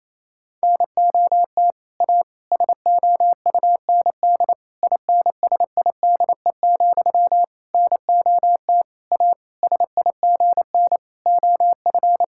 Morse code, heard as a keyed tone, about 28 wpm, 700 Hz. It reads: NOT A SOUND INSIDE, NOT A SIGN OF
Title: Sentences from Top 1000 Words - Review Part 6 - 28wpm